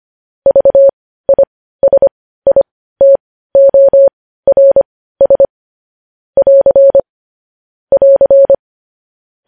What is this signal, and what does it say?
Morse code, heard as a keyed tone, about 25 wpm, spaced as for 15 wpm, 550 Hz. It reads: VISITORS <AR> <AR>